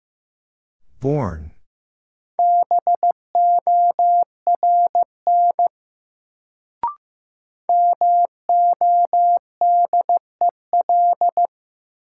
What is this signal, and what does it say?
Morse code, heard as a keyed tone, about 15 wpm, 700 Hz. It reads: BORN MODEL
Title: Sentences from Top 1000 Words - New Word Drill - 15wpm